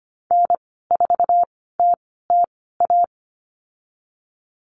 Morse code, heard as a keyed tone, about 25 wpm, spaced as for 16 wpm, 700 Hz. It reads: N4TTA
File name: Lesson 15g - Number 5 - Call Signs 25wpm